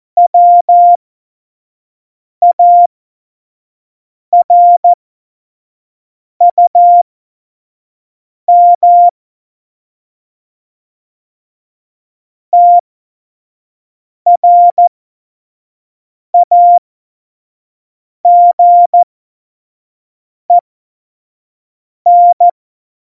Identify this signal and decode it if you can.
Morse code, heard as a keyed tone, about 14 wpm, spaced as for 5 wpm, 700 Hz. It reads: WARUM TRAGEN
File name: Morsekurs / Witze / Witz 20. Oktober 2020, 25 F70